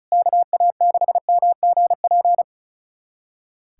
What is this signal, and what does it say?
Morse code, heard as a keyed tone, about 36 wpm, 700 Hz. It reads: KA6MGP
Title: USA Callsign DE Callsign - 6 Characters - 35wpm